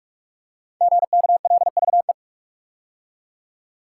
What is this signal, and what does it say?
Morse code, heard as a keyed tone, about 44 wpm, 700 Hz. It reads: GKLVE
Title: Code Groups - 5 Letter:Numbers - 45wpm